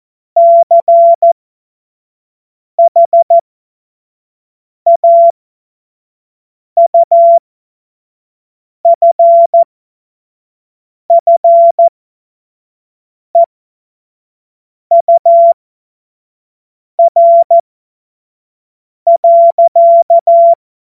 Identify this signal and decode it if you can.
Morse code, heard as a keyed tone, about 14 wpm, spaced as for 5 wpm, 680 Hz. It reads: CHAUFFEUR.
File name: Morsekurs / Witze / Witz 16 Januar 2024 25 F70